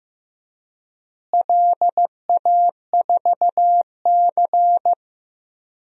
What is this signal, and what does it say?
Morse code, heard as a keyed tone, about 15 wpm, 700 Hz. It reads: LA4C
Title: DXCC-100 Call Signs - 4 Characters - Repeated 2x - 15wpm